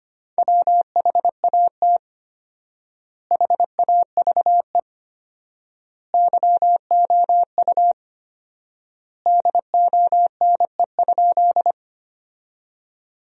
Morse code, heard as a keyed tone, about 25 wpm, 700 Hz. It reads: WHAT HAVE YOU DONE?
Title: 4x Word-Spacing - Sentences from Top 500 Words - 25wpm